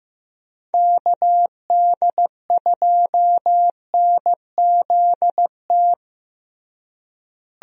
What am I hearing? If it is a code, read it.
Morse code, heard as a keyed tone, about 15 wpm, 700 Hz. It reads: KD2NZT
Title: USA Call Signs - 6 Characters - Repeated 2x - 15wpm